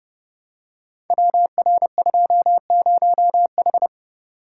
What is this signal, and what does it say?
Morse code, heard as a keyed tone, about 30 wpm, 700 Hz. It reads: WR20H
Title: Code Groups - 5 Letter:Numbers - 30wpm